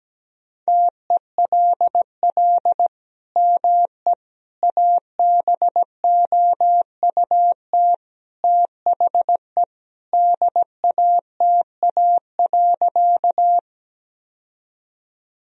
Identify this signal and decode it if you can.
Morse code, heard as a keyed tone, about 17 wpm, 700 Hz. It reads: TELL ME ABOUT THE DATA.
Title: Sentences from Top 1000 Words - Review Part 3 - 17wpm